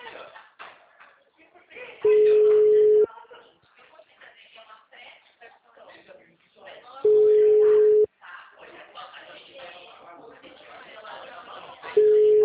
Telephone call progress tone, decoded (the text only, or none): ringback tone